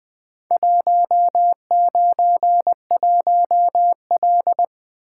Morse code, heard as a keyed tone, about 20 wpm, 700 Hz. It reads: 191L
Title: Four Letter-Number Combinations - 20wpm